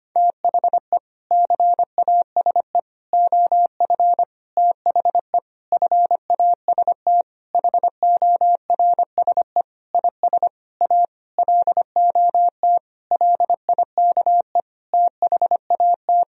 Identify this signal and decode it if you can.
Morse code, heard as a keyed tone, about 25 wpm, 700 Hz. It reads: THE CASE OF THE FAST HORSE IS A LOT LIKE THAT
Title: Sentences from Top 1000 Words - Review Part 2 - 25wpm